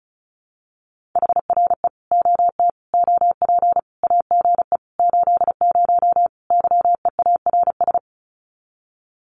Morse code, heard as a keyed tone, about 34 wpm, 700 Hz. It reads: HRE OT OP AGE 80 YEARS
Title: Common QSO Phrases 35wpm